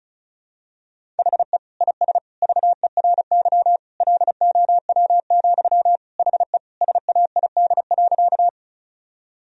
Morse code, heard as a keyed tone, about 34 wpm, 700 Hz. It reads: HE IS VERY LOW, HE SAID.
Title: Sentences from Top 1000 Words - Review Part 1 - 35wpm